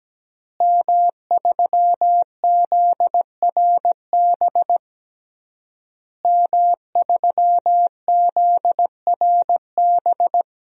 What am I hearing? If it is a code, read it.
Morse code, heard as a keyed tone, about 17 wpm, 700 Hz. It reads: M3ZRB M3ZRB
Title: DXCC-100 Call Signs - 5 Characters - Repeated 3x - 17wpm